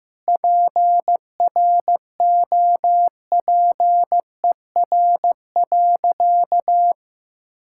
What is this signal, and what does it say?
Morse code, heard as a keyed tone, about 16 wpm, 700 Hz. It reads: PROPER.
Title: Sentences from Top 1000 Words - Review Part 5 - 15wpm